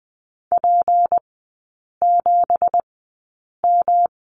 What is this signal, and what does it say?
Morse code, heard as a keyed tone, about 20 wpm, spaced as for 8 wpm, 700 Hz. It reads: P7M
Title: 166 Lesson 27e - Number 7 - Three Characters 20wpm